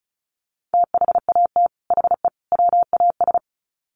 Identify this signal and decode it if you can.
Morse code, heard as a keyed tone, about 34 wpm, 700 Hz. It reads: THAT HE WAS
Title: Sets of 3 Words 35wpm